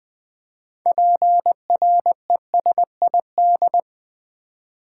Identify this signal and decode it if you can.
Morse code, heard as a keyed tone, about 20 wpm, 700 Hz. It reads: PRESID
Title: Build A Word - 6 Letters At A Time - 12-Letter Words - 20wpm